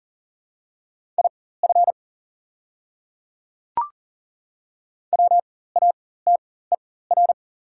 Morse code, heard as a keyed tone, about 40 wpm, spaced as for 19 wpm, 700 Hz. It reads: IF WATER
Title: 4x Character Spacing - Top 100 Words - 40wpm